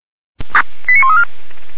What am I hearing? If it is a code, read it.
Five-tone selective call (ZVEI), heard as quick runs of five tones, tones of about 70 ms. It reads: 89134
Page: /464/SDR/ZVEI/89109